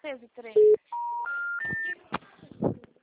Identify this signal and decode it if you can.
Special information tone, with speech.
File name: out- -2005- - - .67141